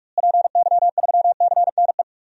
Morse code, heard as a keyed tone, about 46 wpm, 700 Hz. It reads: PY3CNE
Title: DXCC-100 Call Signs - 6 Characters - Repeated 3x - 45wpm